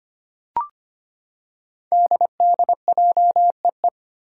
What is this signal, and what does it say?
Morse code, heard as a keyed tone, about 25 wpm, 700 Hz. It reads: DDJEE
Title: Five Challenging Letter-Number Combinations 25wpm